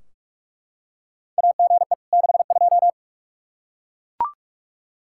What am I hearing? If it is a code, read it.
Morse code, heard as a keyed tone, about 43 wpm, 700 Hz. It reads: AGE 62
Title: K1USN SST QSO Practice 45wpm